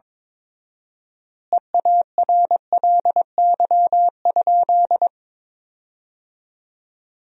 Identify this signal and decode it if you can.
Morse code, heard as a keyed tone, about 22 wpm, 700 Hz. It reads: EARLY?
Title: 4x Word-Spacing - Sentences from Top 400 Words - 22wpm